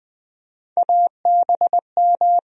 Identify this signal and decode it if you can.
Morse code, heard as a keyed tone, about 20 wpm, 700 Hz. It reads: ABM
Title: USA Call Sign Postfixes - 3 Letters 20wpm